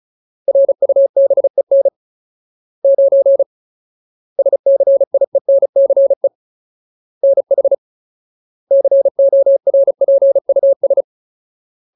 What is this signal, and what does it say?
Morse code, heard as a keyed tone, about 35 wpm, 550 Hz. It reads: RUBEN 9 SCIENCE NH CORPUS
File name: Ultimate Word List / 35WPM Ultimate